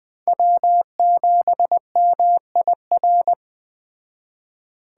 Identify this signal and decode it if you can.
Morse code, heard as a keyed tone, about 21 wpm, 700 Hz. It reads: W7MIR